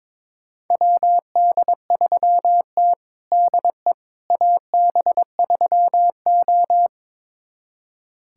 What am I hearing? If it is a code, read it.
Morse code, heard as a keyed tone, about 21 wpm, 700 Hz. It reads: WD3T DE AB3O
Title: USA Callsign DE Callsign - 4 Characters - 22wpm